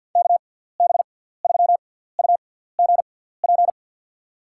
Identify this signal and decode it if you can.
Morse code, heard as a keyed tone, about 52 wpm, spaced as for 17 wpm, 700 Hz. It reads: KB3UGP
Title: USA Call Signs - 6 Characters - 6x Character Spacing - 50wpm